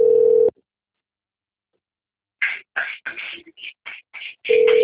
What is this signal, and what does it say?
Ringback tone (North America), 440 + 480 Hz.
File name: rg-1600-unknown- - - .41611